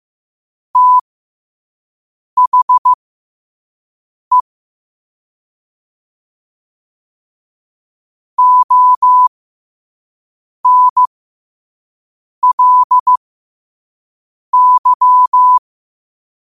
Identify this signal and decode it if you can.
Morse code, heard as a keyed tone, about 15 wpm, spaced as for 5 wpm, 1000 Hz. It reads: THE ONLY
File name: qotdfiles / 5wpm